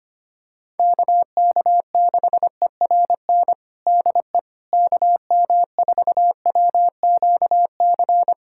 Morse code, heard as a keyed tone, about 25 wpm, 700 Hz. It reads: KK6ERN DE KM4WQC
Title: USA Callsign DE Callsign - 6 Characters - 25wpm